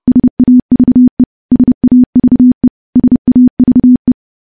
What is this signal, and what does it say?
Morse code, heard as a keyed tone, about 30 wpm, 250 Hz. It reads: SAVE SAVE SAVE
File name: morse / morse-training-30WPM